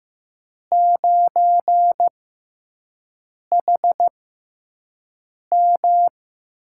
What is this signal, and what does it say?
Morse code, heard as a keyed tone, about 15 wpm, spaced as for 5 wpm, 700 Hz. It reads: 9HM
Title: USA Call Signs - 6x Character Spacing - 15wpm